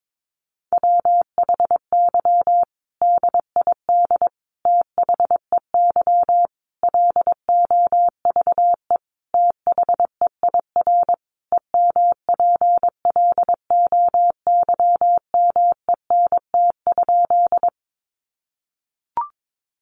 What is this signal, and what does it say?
Morse code, heard as a keyed tone, about 22 wpm, 700 Hz. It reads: WHY DID THEY LOVE THEIR EMPLOYMENT?
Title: Sentences from Top 1500 Words 22wpm